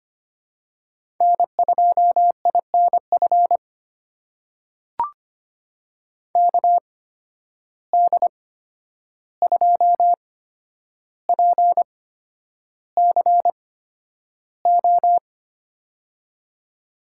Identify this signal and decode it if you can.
Morse code, heard as a keyed tone, about 25 wpm, spaced as for 10 wpm, 700 Hz. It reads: N2INF KD2PCO
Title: USA Call Signs - 8x Character Spacing - 25wpm